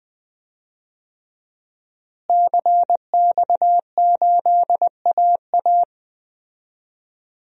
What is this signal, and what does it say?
Morse code, heard as a keyed tone, about 20 wpm, 700 Hz. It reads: CX8AA
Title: DXCC-100 Call Signs - 5 Characters - Repeated 3x - 20wpm